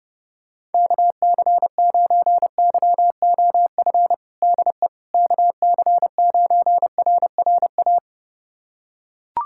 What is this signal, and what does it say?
Morse code, heard as a keyed tone, about 30 wpm, 700 Hz. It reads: KC9YOF DE KC9RRA
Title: USA Callsign DE Callsign 30wpm